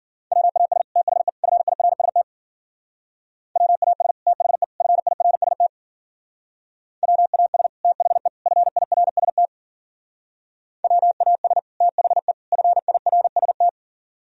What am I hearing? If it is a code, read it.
Morse code, heard as a keyed tone, about 52 wpm, 700 Hz. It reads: WAS THE FIRST WAS THE FIRST WAS THE FIRST WAS THE FIRST